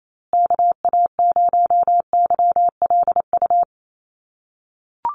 Morse code, heard as a keyed tone, about 28 wpm, 700 Hz. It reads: KA0YLU
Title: USA Call Signs Repeated 2x 28wpm